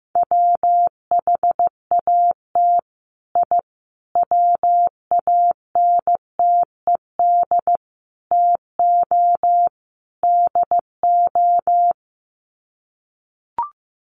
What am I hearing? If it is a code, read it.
Morse code, heard as a keyed tone, about 15 wpm, 700 Hz. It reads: WHAT I WANTED TO DO
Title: Sets of 5 Words Encore2 15wpm